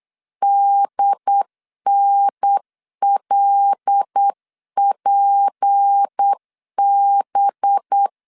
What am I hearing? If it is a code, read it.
Morse code, heard as a keyed tone, about 8 wpm, 800 Hz. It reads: DNLPB
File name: LEZIONI / 7 GF9Z / 7 G 8wpm 3118